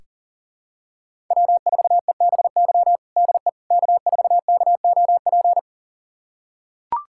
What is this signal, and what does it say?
Morse code, heard as a keyed tone, about 37 wpm, 700 Hz. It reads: W4EBY DE K4KOP
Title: USA Callsign DE Callsign - 5 Characters - 40wpm